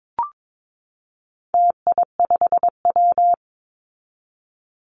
Morse code, heard as a keyed tone, about 22 wpm, 700 Hz. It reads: TI5W
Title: Call Signs Repeated 2x - 22wpm